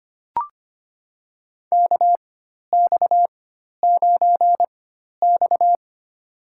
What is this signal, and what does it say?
Morse code, heard as a keyed tone, about 25 wpm, spaced as for 12 wpm, 700 Hz. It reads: KX9X
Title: USA Call Signs - 4 Characters - 4x Character Spacing - 25wpm